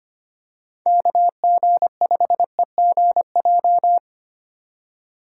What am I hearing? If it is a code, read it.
Morse code, heard as a keyed tone, about 25 wpm, 700 Hz. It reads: KG5EGJ
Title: USA Call Signs - 8x Character Spacing - 25wpm